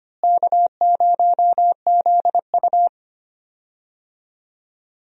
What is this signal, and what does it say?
Morse code, heard as a keyed tone, about 25 wpm, 700 Hz. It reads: K0ZU
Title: USA Call Signs - 4 Characters - Repeated 2x - 25wpm